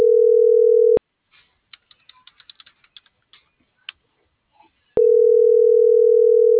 Ringback tone (North America), 440 + 480 Hz.